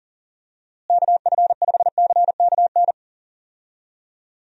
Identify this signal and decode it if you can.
Morse code, heard as a keyed tone, about 40 wpm, 700 Hz. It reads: KF5CKN
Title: USA Call Signs - 6 Characters - Repeated 2x - 40wpm